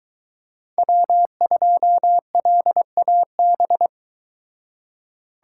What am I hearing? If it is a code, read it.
Morse code, heard as a keyed tone, about 23 wpm, 700 Hz. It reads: W2LAB